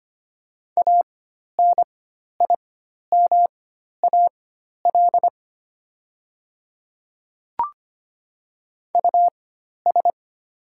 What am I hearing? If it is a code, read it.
Morse code, heard as a keyed tone, about 25 wpm, spaced as for 12 wpm, 700 Hz. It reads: ANIMAL US
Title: 4x Character Spacing - Top 300 Words - 25wpm